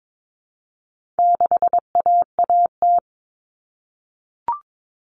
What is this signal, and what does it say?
Morse code, heard as a keyed tone, about 22 wpm, 700 Hz. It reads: 6AAT
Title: Common Cut Numbers - 4-Digit 22wpm